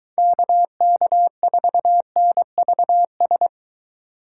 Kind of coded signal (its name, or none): Morse code